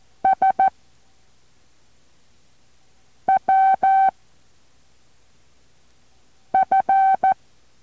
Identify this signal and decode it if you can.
Morse code, heard as a keyed tone, about 14 wpm, 770 Hz. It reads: S W F
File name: intl morse code2